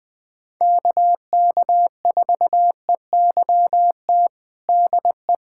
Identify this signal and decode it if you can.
Morse code, heard as a keyed tone, about 20 wpm, 700 Hz. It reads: KK4EYT DE
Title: USA Callsign DE Callsign - 6 Characters - 20wpm